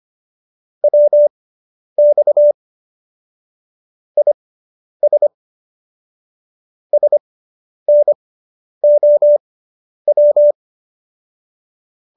Morse code, heard as a keyed tone, about 25 wpm, spaced as for 10 wpm, 590 Hz. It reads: WX IS SNOW